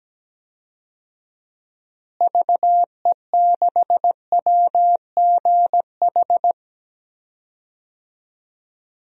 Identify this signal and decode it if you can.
Morse code, heard as a keyed tone, about 17 wpm, 700 Hz. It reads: VE6WGH
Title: DXCC-100 Call Signs - 6 Characters - Repeated 2x - 17wpm